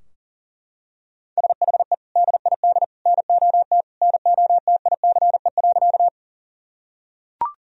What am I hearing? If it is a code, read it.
Morse code, heard as a keyed tone, about 40 wpm, 700 Hz. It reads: SHE DID NOT NOTICE.
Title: Sentences from Top 1000 Words - Review Part 2 - 40wpm